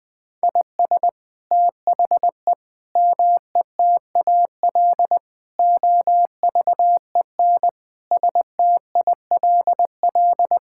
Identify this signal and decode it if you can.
Morse code, heard as a keyed tone, about 20 wpm, 700 Hz. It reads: IS THE METAL OVEN STILL
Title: Sentences from Top 1000 Words - Review Part 6 - 20wpm